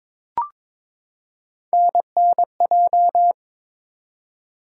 Morse code, heard as a keyed tone, about 22 wpm, 700 Hz. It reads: NNJ
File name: ARRL Sections v2 - 22wpm